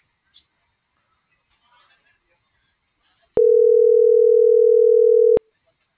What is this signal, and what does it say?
Ringback tone (North America), 440 + 480 Hz.